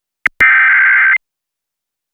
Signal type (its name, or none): Emergency Alert System (SAME) tones